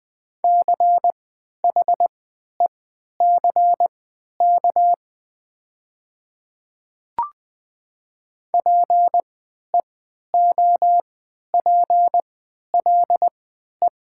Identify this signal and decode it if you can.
Morse code, heard as a keyed tone, about 20 wpm, spaced as for 11 wpm, 700 Hz. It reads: CHECK PEOPLE
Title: 3x Character Spacing - Top 500 Words - 20wpm